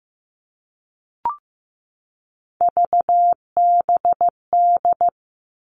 Morse code, heard as a keyed tone, about 15 wpm, 700 Hz. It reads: VBD
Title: All Cut Numbers - 3-Digit 15wpm